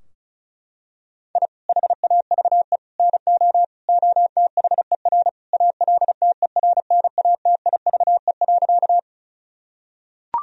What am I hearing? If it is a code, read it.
Morse code, heard as a keyed tone, about 36 wpm, 700 Hz. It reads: I HAVE NO OTHER ALTERNATIVE.
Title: Sentences from Top 1000 Words - Review Part 4 - 35wpm